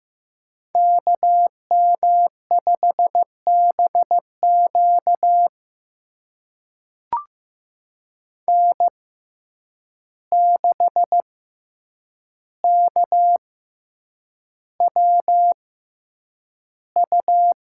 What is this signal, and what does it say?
Morse code, heard as a keyed tone, about 15 wpm, spaced as for 8 wpm, 700 Hz. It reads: KM5BQ N6KWU
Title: USA Call Signs - 5 Characters - 6x Character Spacing - 15wpm